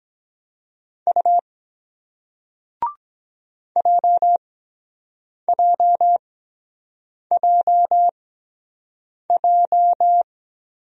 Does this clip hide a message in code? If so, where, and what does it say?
Yes: Morse code from 1.1 to 10.2 s, heard as a keyed tone, about 22 wpm, spaced as for 7 wpm, 700 Hz. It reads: U JJJJ